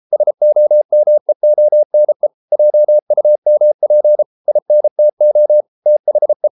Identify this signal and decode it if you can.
Morse code, heard as a keyed tone, about 33 wpm, 600 Hz. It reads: SOMEONE JUMP INTO THE